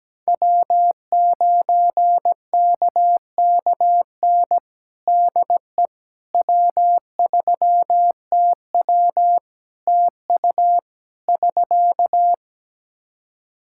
Morse code, heard as a keyed tone, about 17 wpm, 700 Hz. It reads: W9KKN DE W3TW TU <SK>
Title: Practice QSO - Exchange 4 - Ending - 17wpm